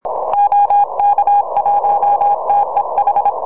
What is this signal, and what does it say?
Morse code, heard as a keyed tone, about 26 wpm, 810 Hz, with noise. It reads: OK1TEH
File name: oh2ma 4m